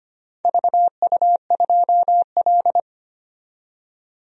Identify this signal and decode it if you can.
Morse code, heard as a keyed tone, about 25 wpm, 700 Hz. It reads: VU2L
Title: DXCC-100 Call Signs - 4 Characters - 25wpm